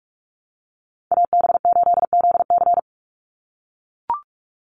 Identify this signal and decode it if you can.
Morse code, heard as a keyed tone, about 44 wpm, 700 Hz. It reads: AB8ZC
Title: USA Call Signs 45wpm